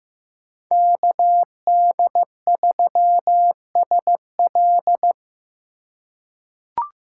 Morse code, heard as a keyed tone, about 15 wpm, 700 Hz. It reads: KD3SL
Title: USA Call Signs - 3x Character Spacing - 15wpm